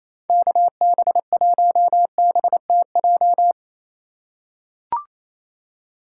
Morse code, heard as a keyed tone, about 28 wpm, 700 Hz. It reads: KB1BTJ
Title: USA Call Signs - 6 Characters - 6x Character Spacing - 28wpm